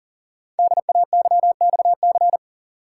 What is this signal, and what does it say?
Morse code, heard as a keyed tone, about 39 wpm, 700 Hz. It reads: DAYXC
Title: Code Groups - 5 Letters - 40wpm